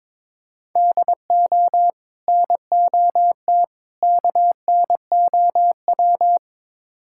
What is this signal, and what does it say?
Morse code, heard as a keyed tone, about 23 wpm, 700 Hz. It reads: DO NOT KNOW
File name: practice / 22wpm / Build A Phrase - 5-Word Phrases - 22wpm